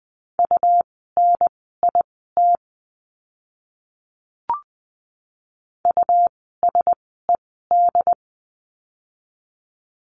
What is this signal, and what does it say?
Morse code, heard as a keyed tone, about 20 wpm, spaced as for 14 wpm, 700 Hz. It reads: UNIT USED
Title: 101 Lesson 17f - Letter u - Words 20wpm